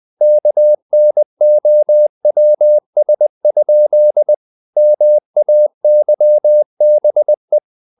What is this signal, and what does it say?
Morse code, heard as a keyed tone, about 20 wpm, 600 Hz. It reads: KNOWS? MAYBE
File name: the old man and the sea / the old man and the sea part 1 20wpm